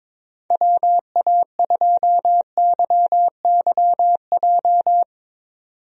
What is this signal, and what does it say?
Morse code, heard as a keyed tone, about 22 wpm, 700 Hz. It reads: WA2YYJ
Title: USA Call Signs - 6 Characters - 22wpm